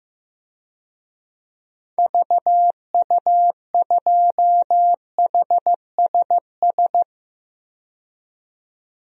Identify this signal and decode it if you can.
Morse code, heard as a keyed tone, about 15 wpm, 700 Hz. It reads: VU2HSS
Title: DXCC-100 Call Signs - 6 Characters - 15wpm